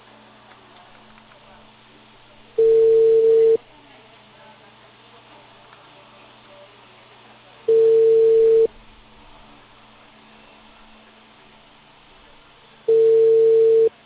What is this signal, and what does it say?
Ringback tone (China, 450 Hz), with speech.